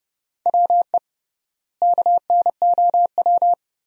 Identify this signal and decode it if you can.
Morse code, heard as a keyed tone, about 30 wpm, 700 Hz. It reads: WE KNOW
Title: 3x Word-Spacing - Sentences from Top 500 Words - 30wpm